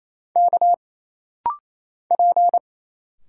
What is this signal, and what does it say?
Morse code, heard as a keyed tone, about 27 wpm, 700 Hz. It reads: K P
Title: Single Letters - Rapid-Fire 28wpm